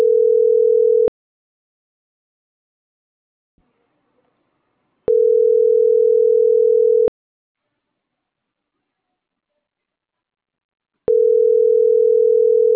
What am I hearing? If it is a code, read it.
Ringback tone (North America), 440 + 480 Hz.